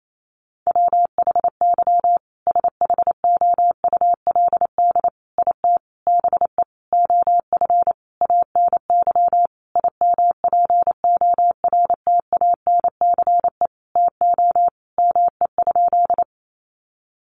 Morse code, heard as a keyed tone, about 28 wpm, 700 Hz. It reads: WHY SHOULD IT BE OF ANY IMPORTANCE TO ME?
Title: Sentences from Top 1200 Words 28wpm